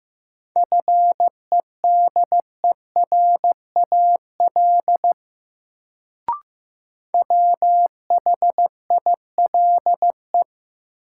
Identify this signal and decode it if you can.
Morse code, heard as a keyed tone, about 15 wpm, 700 Hz. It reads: FEDERAL WHILE
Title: Sets of 3 Words - New Word Drill - 15wpm